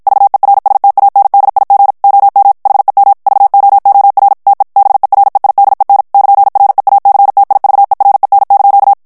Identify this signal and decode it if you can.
Morse code, heard as a keyed tone, about 51 wpm, 800 Hz. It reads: VERSTANDIG OM HEM VOOR TE BEREIDEN <KN>REACTIVEREN)
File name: morse mp3 / HOLLANDAIS texte clair 2 vitesse 50 mots minute